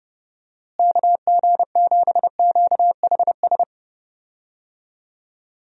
Morse code, heard as a keyed tone, about 30 wpm, 700 Hz. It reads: KG7QHS
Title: USA Call Signs - 6 Characters - Repeated 2x - 30wpm